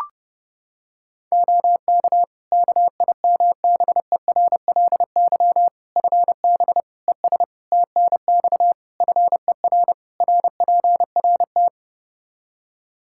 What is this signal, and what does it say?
Morse code, heard as a keyed tone, about 30 wpm, 700 Hz. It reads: OK KIMBERLY FB ES TNX FER RPRT
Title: Practice QSO - Exchange 2 - Info - 30wpm